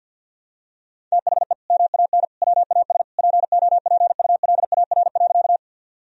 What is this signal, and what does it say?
Morse code, heard as a keyed tone, about 50 wpm, 700 Hz. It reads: THE MAN WAS POPULAR.
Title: Sentences from Top 1000 Words - Review Part 3 - 50wpm